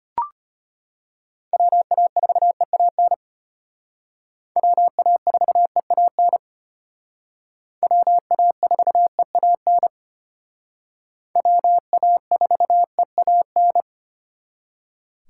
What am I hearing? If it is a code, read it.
Morse code, heard as a keyed tone, about 33 wpm, 700 Hz. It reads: WA4EAN WA4EAN WA4EAN WA4EAN